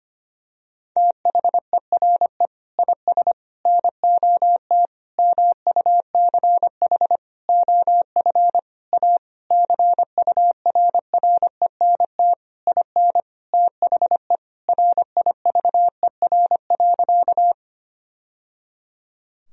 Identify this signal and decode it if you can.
Morse code, heard as a keyed tone, about 25 wpm, 700 Hz. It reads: THERE IS NOT MUCH OF A CURRENT IN THE RIVER.
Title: Sentences from Top 1000 Words - Review Part 4 - 25wpm